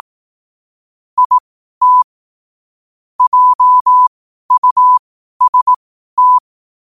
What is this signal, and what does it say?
Morse code, heard as a keyed tone, about 18 wpm, spaced as for 12 wpm, 1000 Hz. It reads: IT JUST
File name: qotdfiles / 12wpm